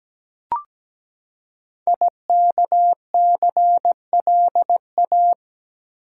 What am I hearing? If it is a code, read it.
Morse code, heard as a keyed tone, about 17 wpm, 700 Hz. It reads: IKCLA
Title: Code Groups - 5 Letter:Numbers - 17wpm